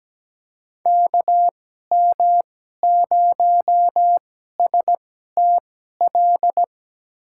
Morse code, heard as a keyed tone, about 17 wpm, spaced as for 12 wpm, 700 Hz. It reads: KM0STL
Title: USA Call Signs - 6 Characters - 2x Character Spacing - 17wpm